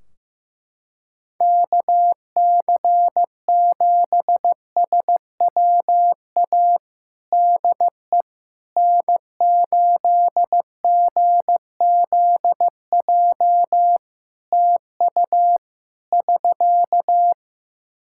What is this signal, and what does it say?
Morse code, heard as a keyed tone, about 16 wpm, 700 Hz. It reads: KC7SWA DE N8GZJ TU <SK>
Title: Practice QSO - Exchange 4 - Ending - 15wpm